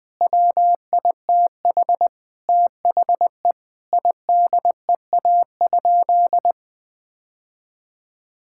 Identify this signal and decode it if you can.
Morse code, heard as a keyed tone, about 20 wpm, 700 Hz. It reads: WITH THE IDEA?
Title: Sentences with 7-Word Phrases - Spoken Beginning - 20wpm